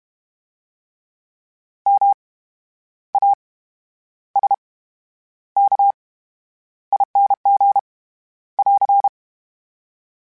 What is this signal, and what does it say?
Morse code, heard as a keyed tone, about 32 wpm, spaced as for 11 wpm, 800 Hz. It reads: MASKING <AR>
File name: Special INGs / Special INGs-mp3s / VerySlowToFaster Ing @30